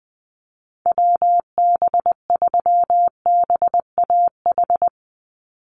Morse code, heard as a keyed tone, about 20 wpm, 700 Hz. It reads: WB3BAH
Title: USA Call Signs 20wpm